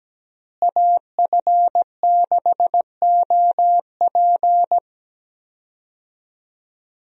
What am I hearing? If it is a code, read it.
Morse code, heard as a keyed tone, about 17 wpm, 700 Hz. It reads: AF6OP
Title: USA Call Signs - 5 Characters - Repeated 2x - 17wpm